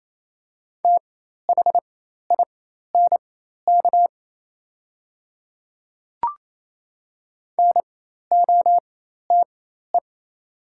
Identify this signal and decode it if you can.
Morse code, heard as a keyed tone, about 28 wpm, spaced as for 13 wpm, 700 Hz. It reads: THINK NOTE
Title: 4x Character Spacing - Top 500 Words - 28wpm